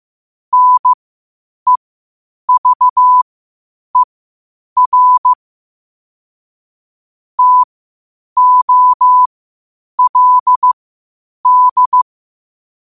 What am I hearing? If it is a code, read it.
Morse code, heard as a keyed tone, about 15 wpm, spaced as for 8 wpm, 1000 Hz. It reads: NEVER TOLD
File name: qotdfiles / 8wpm